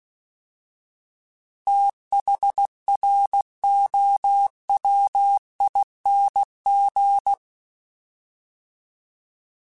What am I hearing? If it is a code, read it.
Morse code, heard as a keyed tone, about 16 wpm, 800 Hz. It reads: THROWING
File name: TH / TH mp3s / Fast TH @15